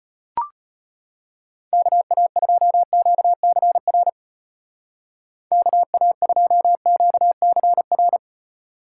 Morse code, heard as a keyed tone, about 31 wpm, 700 Hz. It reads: KA2QCR KA2QCR
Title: Speed Racing - USA 2x3 Call Signs 25wpm